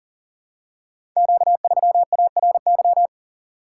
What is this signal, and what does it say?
Morse code, heard as a keyed tone, about 41 wpm, 700 Hz. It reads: Q3ARY
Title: Code Groups - 5 Letter:Numbers - 40wpm